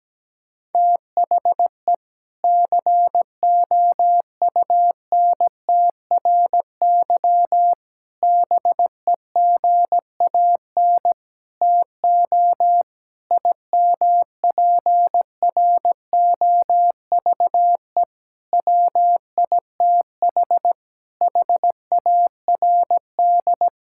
Morse code, heard as a keyed tone, about 17 wpm, 700 Hz. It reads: THE COUNTRY BEGAN TO IMPROVE WITH HARD
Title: Sentences from Top 1000 Words - Review Part 5 - 17wpm